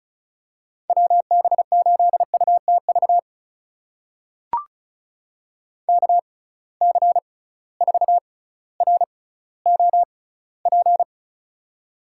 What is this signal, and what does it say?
Morse code, heard as a keyed tone, about 36 wpm, spaced as for 18 wpm, 700 Hz. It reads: WB8UTV KC4ROP
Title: USA Call Signs - 6 Characters - 6x Character Spacing - 35wpm